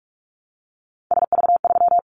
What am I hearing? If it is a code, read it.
Morse code, heard as a keyed tone, about 44 wpm, 700 Hz. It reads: SV3
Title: Callsign Prefixes for Top 50 DXCC - 45wpm